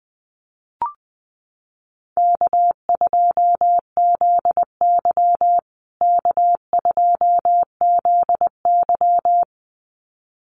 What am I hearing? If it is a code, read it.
Morse code, heard as a keyed tone, about 20 wpm, 700 Hz. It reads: K2ZY K2ZY
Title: USA Call Signs Repeated 2x 20wpm